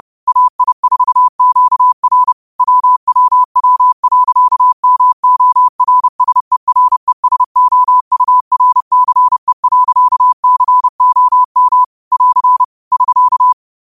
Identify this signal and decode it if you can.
Morse code, heard as a keyed tone, about 30 wpm, 1000 Hz. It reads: AI4QR WWW.MORSERESOURCE.COM <AR> <SK>